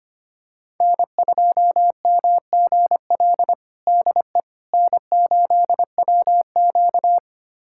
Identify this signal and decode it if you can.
Morse code, heard as a keyed tone, about 25 wpm, 700 Hz. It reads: N2MGL DE N8WQ
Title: USA Callsign DE Callsign 25wpm